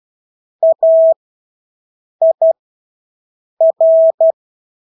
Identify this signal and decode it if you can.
Morse code, heard as a keyed tone, about 12 wpm, spaced as for 6 wpm, 650 Hz. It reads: AIR
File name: Engelske ord 12-6 wpm